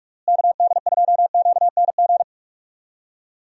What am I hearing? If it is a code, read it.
Morse code, heard as a keyed tone, about 46 wpm, 700 Hz. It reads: KD2QNG
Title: USA Call Signs - 6 Characters - Repeated 3x - 45wpm